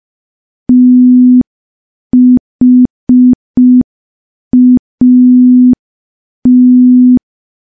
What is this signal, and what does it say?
Morse code, heard as a keyed tone, about 5 wpm, 250 Hz. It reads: THAT
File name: morse / morse-training-05WPM